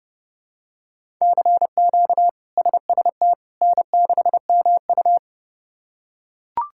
Morse code, heard as a keyed tone, about 32 wpm, 700 Hz. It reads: CQ SST N6MU
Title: K1USN SST QSO Practice - 30wpm